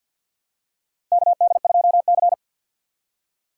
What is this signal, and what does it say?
Morse code, heard as a keyed tone, about 51 wpm, 700 Hz. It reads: KD2C
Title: USA Call Signs - 3x Character Spacing - 50wpm